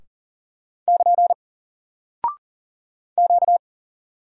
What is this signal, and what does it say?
Morse code, heard as a keyed tone, about 40 wpm, 700 Hz. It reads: <KN> Q
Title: Single Character (Letter:Number:Punctuation:Prosign) 40wpm